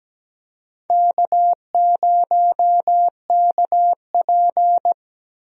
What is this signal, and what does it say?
Morse code, heard as a keyed tone, about 17 wpm, 700 Hz. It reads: K0KP
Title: USA Call Signs - 4 Characters - Repeated 3x - 17wpm